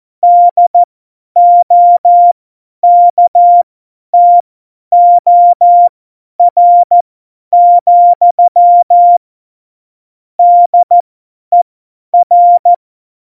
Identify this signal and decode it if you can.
Morse code, heard as a keyed tone, about 14 wpm, spaced as for 10 wpm, 700 Hz. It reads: DOKTOR, DER
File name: Morsekurs / Witze / Witz 01. September 2020, 50 F70